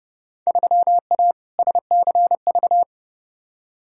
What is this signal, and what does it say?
Morse code, heard as a keyed tone, about 31 wpm, 700 Hz. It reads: 3A SCV
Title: Field Day Exchanges - 30wpm